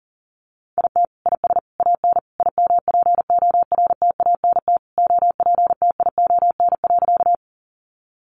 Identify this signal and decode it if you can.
Morse code, heard as a keyed tone, about 40 wpm, 700 Hz. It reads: IT IS AN IMPORTANT OPTION.
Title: Sentences from Top 1300 Words 40wpm